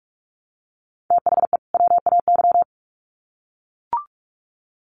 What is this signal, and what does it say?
Morse code, heard as a keyed tone, about 44 wpm, 700 Hz. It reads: THE WAY
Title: Sets of 2 Words 45wpm